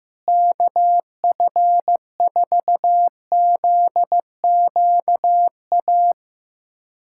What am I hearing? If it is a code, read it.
Morse code, heard as a keyed tone, about 15 wpm, 700 Hz. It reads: KF4ZQA
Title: USA Callsign DE Callsign 15wpm